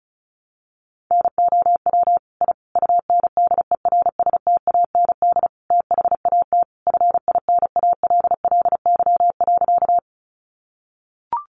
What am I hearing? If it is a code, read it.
Morse code, heard as a keyed tone, about 34 wpm, 700 Hz. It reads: NOW I UNDERSTAND THAT FINALLY.
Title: Sentences from Top 1000 Words 35wpm